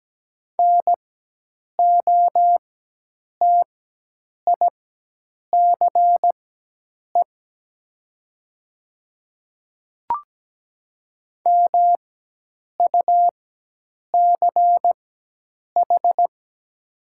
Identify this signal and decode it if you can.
Morse code, heard as a keyed tone, about 17 wpm, spaced as for 8 wpm, 700 Hz. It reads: NOTICE MUCH
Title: 4x Character Spacing - Top 500 Words - 17wpm